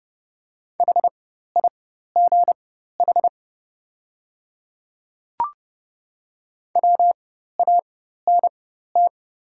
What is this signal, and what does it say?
Morse code, heard as a keyed tone, about 30 wpm, spaced as for 14 wpm, 700 Hz. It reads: HIGH WANT
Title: 4x Character Spacing - Top 200 Words - 30wpm